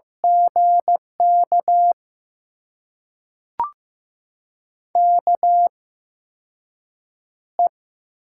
Morse code, heard as a keyed tone, about 15 wpm, spaced as for 6 wpm, 700 Hz. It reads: GK KE
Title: USA Call Signs - 6 Characters - 8x Character Spacing - 15wpm